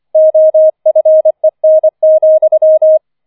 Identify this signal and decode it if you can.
Morse code, heard as a keyed tone, about 25 wpm, 630 Hz. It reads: OFEN,
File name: 004 Vierter Streich 25wpm